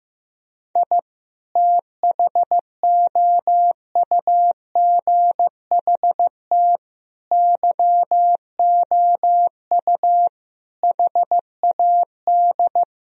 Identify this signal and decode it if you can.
Morse code, heard as a keyed tone, about 15 wpm, 700 Hz. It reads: I THOUGHT YOU HAD
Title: Sentences from Top 1000 Words - Review Part 4 - 15wpm